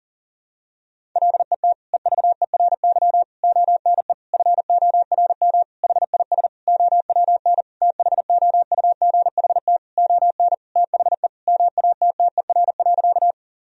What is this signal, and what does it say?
Morse code, heard as a keyed tone, about 40 wpm, 700 Hz. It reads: LET EVERY ONE FORM HIS OWN THOUGHT ON THE MATTER.
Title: Sentences from Top 1000 Words - Review Part 1 - 40wpm